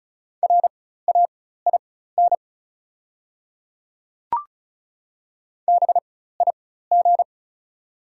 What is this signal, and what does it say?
Morse code, heard as a keyed tone, about 36 wpm, spaced as for 16 wpm, 690 Hz. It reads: RAIN BIG
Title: 4x Character Spacing - Top 500 Words - 35wpm